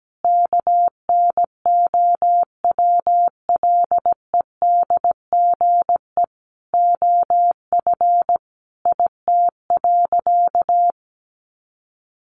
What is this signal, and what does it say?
Morse code, heard as a keyed tone, about 17 wpm, 700 Hz. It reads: KNOWLEDGE OF IT.
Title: Sentences from Top 1400 Words 17wpm